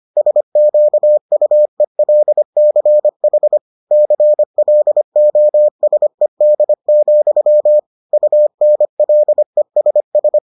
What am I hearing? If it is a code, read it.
Morse code, heard as a keyed tone, about 25 wpm, 600 Hz. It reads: SQUELCH CLOSED, UNLESS